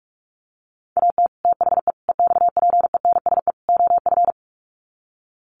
Morse code, heard as a keyed tone, about 44 wpm, 700 Hz. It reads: AT THE EXPENSE OF
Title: Sets of 4 Words 45wpm